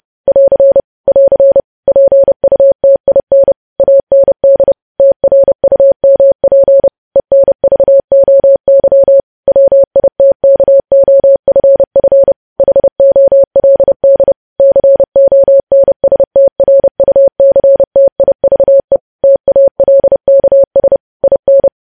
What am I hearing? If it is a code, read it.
Morse code, heard as a keyed tone, about 30 wpm, 550 Hz. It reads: <AR> <AR> PUTIN AND TRUMP ENVOY WITKOFF HOLD CONSTRUCTIVE TALKS IN
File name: podcast / News headlines-1x-30x30